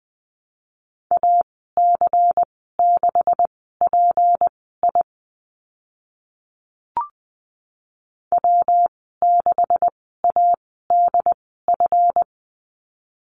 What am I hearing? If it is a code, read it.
Morse code, heard as a keyed tone, about 19 wpm, spaced as for 14 wpm, 700 Hz. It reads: AC6PI W6ADF
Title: 186 Lesson 30g - Letter b - Call Signs 20wpm